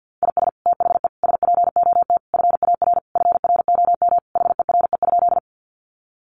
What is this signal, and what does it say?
Morse code, heard as a keyed tone, about 49 wpm, 700 Hz. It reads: IS THE SPOT FAR FROM HERE?
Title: Sentences from Top 1100 Words 50wpm